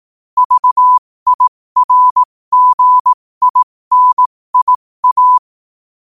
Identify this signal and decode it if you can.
Morse code, heard as a keyed tone, about 18 wpm, spaced as for 15 wpm, 1000 Hz. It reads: VIRGINIA
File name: qotdfiles / 15wpm